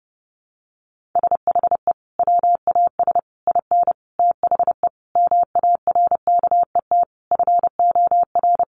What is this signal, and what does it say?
Morse code, heard as a keyed tone, about 30 wpm, 700 Hz. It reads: SHE WAS IN THE MARKET FOR
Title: Sentences from Top 600 Words 30wpm